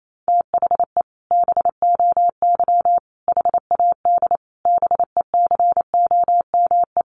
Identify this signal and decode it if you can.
Morse code, heard as a keyed tone, about 28 wpm, 700 Hz. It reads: THE BOY HAD BECOME
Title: Sentences from Top 1500 Words 28wpm